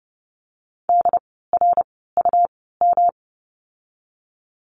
Morse code, heard as a keyed tone, about 30 wpm, spaced as for 17 wpm, 700 Hz. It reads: DRUM
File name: course / 30:10wpm / 108 Lesson 18f - Letter m - Words 30wpm